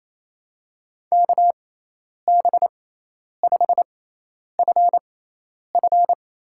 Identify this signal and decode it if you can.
Morse code, heard as a keyed tone, about 28 wpm, spaced as for 10 wpm, 700 Hz. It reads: KB5FF
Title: USA Call Signs - 6 Characters - 6x Character Spacing - 28wpm